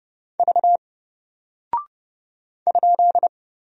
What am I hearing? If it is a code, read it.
Morse code, heard as a keyed tone, about 30 wpm, 700 Hz. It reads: V ?
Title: Single Character (Letter:Number:Punctuation:Prosign) Alt 30wpm